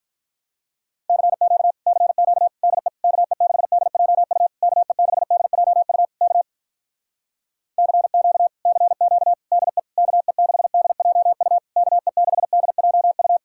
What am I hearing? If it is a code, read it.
Morse code, heard as a keyed tone, about 46 wpm, 700 Hz. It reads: CQ CQ DE KE6DJU KE6DJU K CQ CQ DE KE6DJU KE6DJU